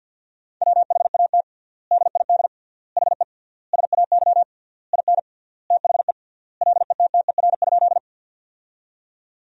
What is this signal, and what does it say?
Morse code, heard as a keyed tone, about 50 wpm, 700 Hz. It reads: WHAT DID HE SAY IN THE LETTER?
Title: 3x Word-Spacing - Sentences from Top 300 Words - 50wpm